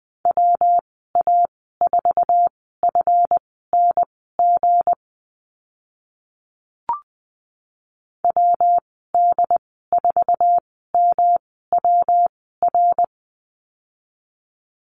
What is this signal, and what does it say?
Morse code, heard as a keyed tone, about 20 wpm, spaced as for 14 wpm, 700 Hz. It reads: WA4FNG WD4MWR
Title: 193 Lesson 31g - Letter v - Call Signs 20wpm